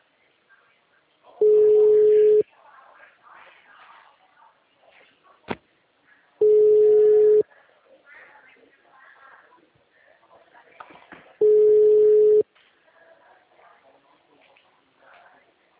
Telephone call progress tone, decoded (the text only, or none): ringback tone